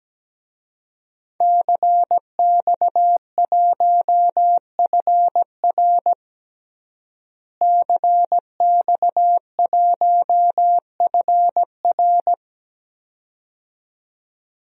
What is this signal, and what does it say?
Morse code, heard as a keyed tone, about 17 wpm, 700 Hz. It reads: CX1FR CX1FR
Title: DXCC-100 Call Signs - 5 Characters - Repeated 2x - 17wpm